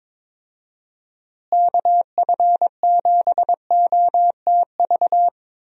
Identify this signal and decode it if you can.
Morse code, heard as a keyed tone, about 21 wpm, 700 Hz. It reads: KF7OTV
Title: USA Call Signs - 6 Characters - Repeated 2x - 22wpm